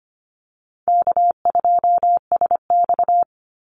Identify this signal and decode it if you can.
Morse code, heard as a keyed tone, about 25 wpm, 700 Hz. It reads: K2SX
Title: Call Signs Repeated 3x - 25wpm